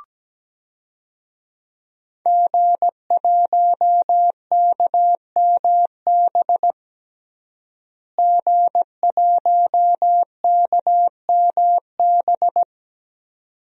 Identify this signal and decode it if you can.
Morse code, heard as a keyed tone, about 17 wpm, 700 Hz. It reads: G1KMB G1KMB
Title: DXCC-100 Call Signs - 5 Characters - Repeated 2x - 17wpm